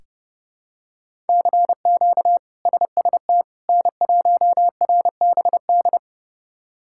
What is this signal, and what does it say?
Morse code, heard as a keyed tone, about 30 wpm, 700 Hz. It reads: CQ SST N1RBD
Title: K1USN SST QSO Practice - 30wpm